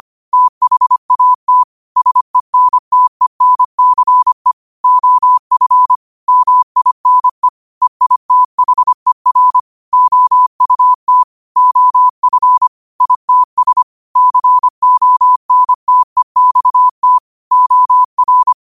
Morse code, heard as a keyed tone, about 25 wpm, 1000 Hz. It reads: THAT SENTENCE OF MINE EITHER OUT OF ITS CONTEXT OR
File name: qotdfiles / 25wpm